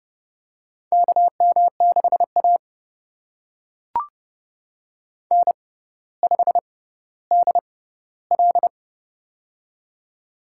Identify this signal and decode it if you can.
Morse code, heard as a keyed tone, about 30 wpm, spaced as for 15 wpm, 700 Hz. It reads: KM6A N5DL
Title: USA Call Signs - 4 Characters - 6x Character Spacing - 30wpm